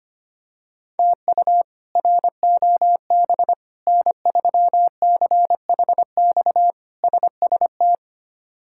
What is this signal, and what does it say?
Morse code, heard as a keyed tone, about 25 wpm, 700 Hz. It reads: TU ROB N3CHX SST
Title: K1USN SST QSO Practice - 25wpm